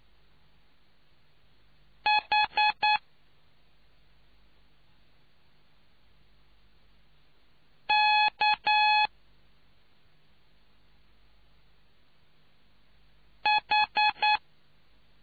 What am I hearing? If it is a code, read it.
Morse code, heard as a keyed tone, about 9 wpm, 840 Hz. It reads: H K H